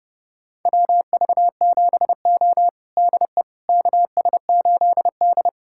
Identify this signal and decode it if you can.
Morse code, heard as a keyed tone, about 30 wpm, 700 Hz. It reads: WV7O DE KS8D
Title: USA Callsign DE Callsign - 4 Characters - 30wpm